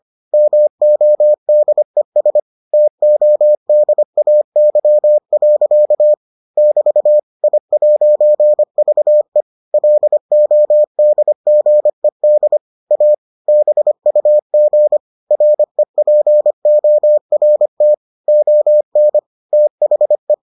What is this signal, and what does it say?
Morse code, heard as a keyed tone, about 25 wpm, 600 Hz. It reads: MODES TODAY. <BT> I'VE LODGED A BUG REPORT ON THE